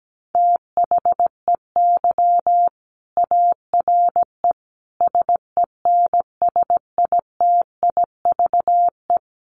Morse code, heard as a keyed tone, about 17 wpm, 700 Hz. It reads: THEY ARE SENSITIVE
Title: Sentences from Top 1400 Words 17wpm